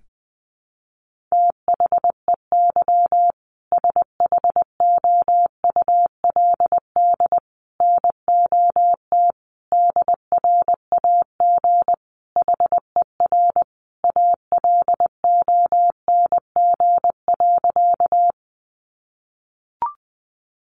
Morse code, heard as a keyed tone, about 20 wpm, 700 Hz. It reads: THEY SHOULD NOT DRAG HER ALONG.
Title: Sentences from Top 1500 Words 20wpm